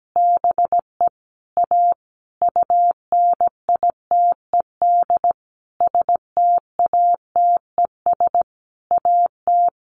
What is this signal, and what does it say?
Morse code, heard as a keyed tone, about 17 wpm, 700 Hz. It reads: BE A UNITED STATES AT
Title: A Sound Of Thunder 17wpm